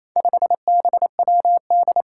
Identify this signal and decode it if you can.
Morse code, heard as a keyed tone, about 28 wpm, 700 Hz. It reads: 5BWD
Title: Four Challenging Letter-Number Combinations 28wpm